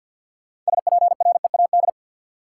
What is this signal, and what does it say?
Morse code, heard as a keyed tone, about 52 wpm, 700 Hz. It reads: SPREAD
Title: Sentences from Top 1200 Words - New Word Drill - 50wpm